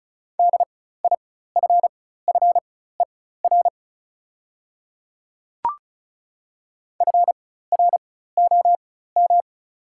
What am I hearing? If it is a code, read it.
Morse code, heard as a keyed tone, about 36 wpm, spaced as for 16 wpm, 700 Hz. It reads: DIFFER FROM
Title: 4x Character Spacing - Top 200 Words - 35wpm